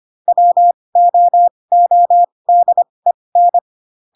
Morse code, heard as a keyed tone, about 25 wpm, spaced as for 20 wpm, 700 Hz. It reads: WOODEN
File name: Pinocchio / Pinocchio ch2 15 25 ik1hge